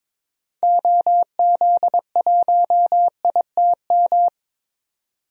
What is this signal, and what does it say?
Morse code, heard as a keyed tone, about 22 wpm, 700 Hz. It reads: OZ1ITM
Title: DXCC-100 Call Signs - 6 Characters - 22wpm